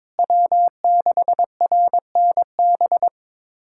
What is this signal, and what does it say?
Morse code, heard as a keyed tone, about 22 wpm, 700 Hz. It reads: W6RNB
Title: USA Callsign DE Callsign 22wpm